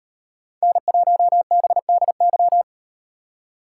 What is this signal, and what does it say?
Morse code, heard as a keyed tone, about 38 wpm, 700 Hz. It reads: N1BDY